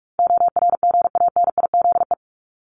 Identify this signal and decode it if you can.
Morse code, heard as a keyed tone, about 44 wpm, 700 Hz. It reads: ORGANIZE
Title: Top 1751-2000 Words 45wpm